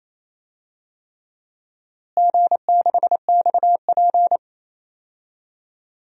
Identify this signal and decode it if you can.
Morse code, heard as a keyed tone, about 28 wpm, 700 Hz. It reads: G6XP
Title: DXCC-100 Call Signs - 4 Characters - Repeated 3x - 28wpm